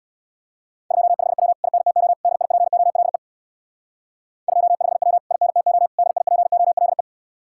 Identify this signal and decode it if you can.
Morse code, heard as a keyed tone, about 62 wpm, 700 Hz. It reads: 25W INTO DIPOLE 25W INTO DIPOLE
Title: Speed-Racing Common QSO Phrases - 50wpm